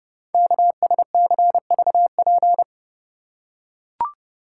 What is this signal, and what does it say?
Morse code, heard as a keyed tone, about 30 wpm, 690 Hz. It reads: KSCVP
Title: Code Groups - 5 Letters - 30wpm